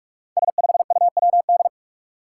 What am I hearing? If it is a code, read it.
Morse code, heard as a keyed tone, about 44 wpm, 700 Hz. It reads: S5UWD
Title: Five Challenging Letter-Number Combinations 45wpm